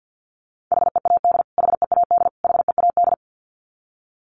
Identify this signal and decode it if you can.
Morse code, heard as a keyed tone, about 50 wpm, 700 Hz. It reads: HEAD HEAD HEAD
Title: Top 500 Words Repeated 50wpm